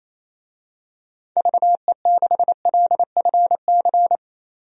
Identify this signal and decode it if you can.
Morse code, heard as a keyed tone, about 27 wpm, 700 Hz. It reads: VE6LFC
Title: DXCC-100 Call Signs - 6 Characters - 28wpm